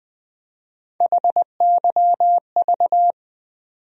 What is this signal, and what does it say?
Morse code, heard as a keyed tone, about 20 wpm, 700 Hz. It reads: HYV
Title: USA Call Sign Postfixes - 3 Letters 20wpm